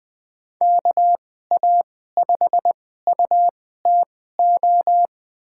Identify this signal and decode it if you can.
Morse code, heard as a keyed tone, about 20 wpm, spaced as for 14 wpm, 700 Hz. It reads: KA5UTO
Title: USA Call Signs - 6 Characters - 2x Character Spacing - 20wpm